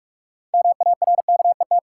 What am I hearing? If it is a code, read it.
Morse code, heard as a keyed tone, about 43 wpm, 700 Hz. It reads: MARKET
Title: Sentences from Top 600 Words - New Word Drill - 45wpm